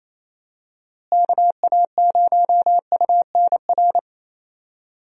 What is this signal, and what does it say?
Morse code, heard as a keyed tone, about 28 wpm, 700 Hz. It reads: KA0UNR
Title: USA Call Signs - 6 Characters - Repeated 2x - 28wpm